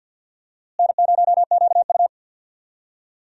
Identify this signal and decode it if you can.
Morse code, heard as a keyed tone, about 46 wpm, 700 Hz. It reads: N0QU
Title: USA Call Signs - 4 Characters - Repeated 2x - 50wpm